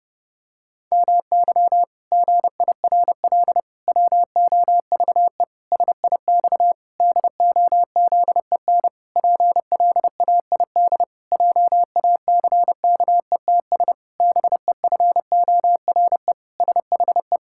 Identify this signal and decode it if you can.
Morse code, heard as a keyed tone, about 30 wpm, 700 Hz. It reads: MY GIRL WOVE SIX DOZEN PLAID JACKETS BEFORE SHE
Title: Pangram Challenge v2 - 30wpm